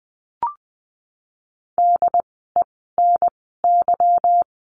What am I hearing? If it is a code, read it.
Morse code, heard as a keyed tone, about 20 wpm, spaced as for 14 wpm, 700 Hz. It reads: DENY
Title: 146 Lesson 24f - Letter y - Words 20wpm